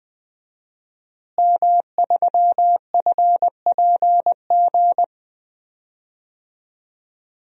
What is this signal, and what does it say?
Morse code, heard as a keyed tone, about 21 wpm, 700 Hz. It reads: M3FPG
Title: DXCC-100 Call Signs - 5 Characters - Repeated 3x - 20wpm